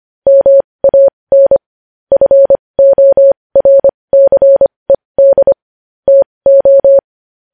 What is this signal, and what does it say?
Morse code, heard as a keyed tone, about 25 wpm, spaced as for 20 wpm, 550 Hz. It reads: MAN FORCED TO
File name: podcast / News headlines-1x-25x20